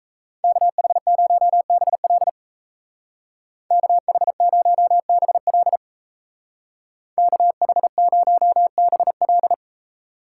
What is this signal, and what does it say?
Morse code, heard as a keyed tone, about 34 wpm, 700 Hz. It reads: KH0BL KH0BL KH0BL
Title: Speed Racing - USA 2x2 Call Signs 28wpm